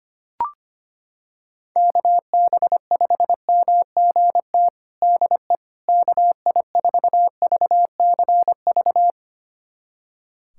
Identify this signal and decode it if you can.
Morse code, heard as a keyed tone, about 25 wpm, 700 Hz. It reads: KB5MGT DE KI4VCV
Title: USA Callsign DE Callsign - 6 Characters - 25wpm